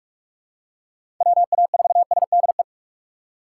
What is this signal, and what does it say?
Morse code, heard as a keyed tone, about 44 wpm, 700 Hz. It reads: WA4SDE
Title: USA Call Signs - 6 Characters - 6x Character Spacing - 45wpm